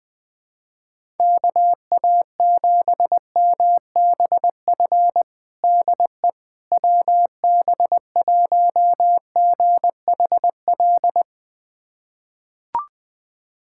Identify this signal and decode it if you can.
Morse code, heard as a keyed tone, about 21 wpm, 700 Hz. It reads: KA7MBF DE WB1GHL
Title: USA Callsign DE Callsign - 6 Characters - 20wpm